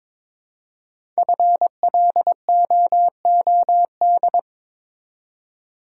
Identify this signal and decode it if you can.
Morse code, heard as a keyed tone, about 22 wpm, 700 Hz. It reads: FLOOD
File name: 5 Letter Words - First 3 Letters Given - 22wpm